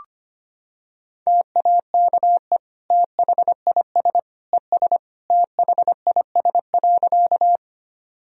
Morse code, heard as a keyed tone, about 25 wpm, 700 Hz. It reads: TAKE THIS ES THIS.
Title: Sentences from Top 200 Words and Abbreviations 25wpm